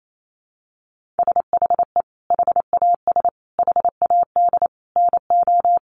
Morse code, heard as a keyed tone, about 28 wpm, 700 Hz. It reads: SHE HAS HAD NO
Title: Sentences from Top 1400 Words 28wpm